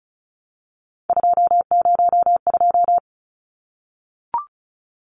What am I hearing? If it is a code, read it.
Morse code, heard as a keyed tone, about 34 wpm, 700 Hz. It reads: 202
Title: Numbers - 3-Digit 35wpm